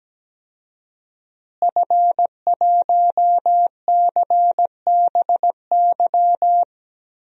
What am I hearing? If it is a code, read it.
Morse code, heard as a keyed tone, about 17 wpm, 700 Hz. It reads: F1CBY
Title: DXCC-100 Call Signs - 5 Characters - Repeated 3x - 17wpm